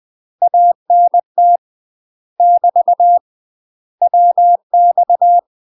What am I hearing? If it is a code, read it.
Morse code, heard as a keyed tone, about 20 wpm, 700 Hz. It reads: ANT <BT> WX
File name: UK MASTER QSOs TRAINING / 20WPM / UK MASTER QSOs 0030